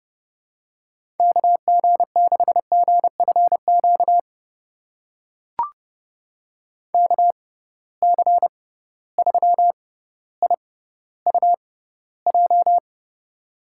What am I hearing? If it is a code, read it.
Morse code, heard as a keyed tone, about 31 wpm, spaced as for 16 wpm, 700 Hz. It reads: KG6GFQ KC3IUJ
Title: USA Call Signs - 6 Characters - 6x Character Spacing - 30wpm